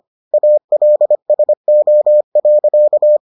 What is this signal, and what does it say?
Morse code, heard as a keyed tone, about 25 wpm, 600 Hz. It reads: ALSO.